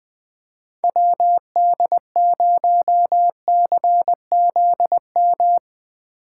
Morse code, heard as a keyed tone, about 20 wpm, 700 Hz. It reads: WD0CZM
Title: Speed Racing - USA 2x3 Call Signs 17wpm